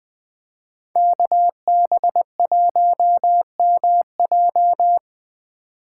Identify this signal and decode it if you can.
Morse code, heard as a keyed tone, about 25 wpm, 700 Hz. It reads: KB1MJ
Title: Speed Racing - USA 2x2 Call Signs 20wpm